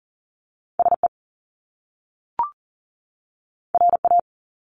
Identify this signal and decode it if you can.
Morse code, heard as a keyed tone, about 40 wpm, 700 Hz. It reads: SE RA
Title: Top Two Letter Combinations in Words 40wpm